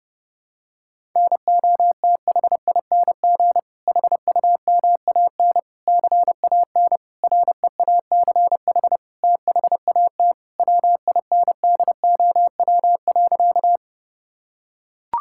Morse code, heard as a keyed tone, about 30 wpm, 700 Hz. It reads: NOTHING HUMAN CAN REACH THAT WINDOW.
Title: Sentences from Top 1600 Words 30wpm